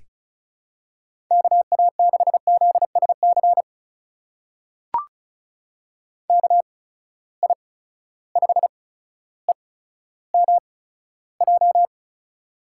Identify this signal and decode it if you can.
Morse code, heard as a keyed tone, about 36 wpm, spaced as for 15 wpm, 700 Hz. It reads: KA6ZSC KI5EMJ
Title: USA Call Signs - 6 Characters - 8x Character Spacing - 35wpm